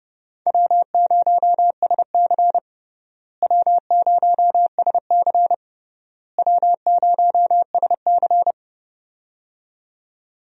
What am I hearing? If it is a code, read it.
Morse code, heard as a keyed tone, about 30 wpm, 700 Hz. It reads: W0SC W0SC W0SC
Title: USA Call Signs - 4 Characters - Repeated 3x - 30wpm